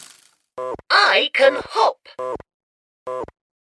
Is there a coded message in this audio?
No, only speech.